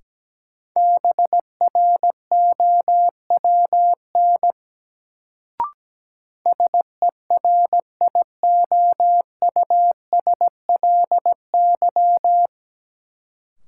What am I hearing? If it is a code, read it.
Morse code, heard as a keyed tone, about 17 wpm, 700 Hz. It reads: BROWN SERIOUSLY
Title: Sentences from Top 1200 Words - New Word Drill Alt - 17wpm